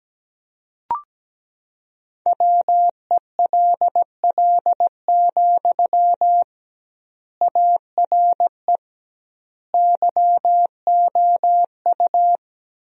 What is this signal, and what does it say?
Morse code, heard as a keyed tone, about 17 wpm, 700 Hz. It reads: WELL, ARE YOU
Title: 2x Word-Spacing - Sentences from Top 400 Words - 17wpm